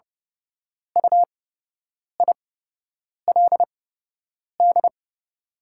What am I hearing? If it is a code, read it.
Morse code, heard as a keyed tone, about 30 wpm, spaced as for 8 wpm, 700 Hz. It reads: UILD
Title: 8x Character Spacing - Top 200 Words - 30wpm